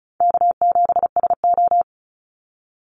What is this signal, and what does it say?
Morse code, heard as a keyed tone, about 34 wpm, 700 Hz. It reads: K7SO
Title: NA SOTA Activator Callsigns 35wpm